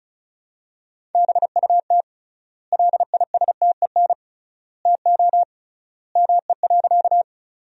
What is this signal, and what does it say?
Morse code, heard as a keyed tone, about 36 wpm, 700 Hz. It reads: BUT LISTEN TO ME.
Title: 3x Word-Spacing - Sentences from Top 400 Words - 35wpm